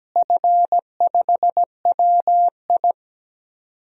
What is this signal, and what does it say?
Morse code, heard as a keyed tone, about 17 wpm, 700 Hz. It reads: F5WI
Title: DXCC-100 Call Signs - 4 Characters - Repeated 3x - 17wpm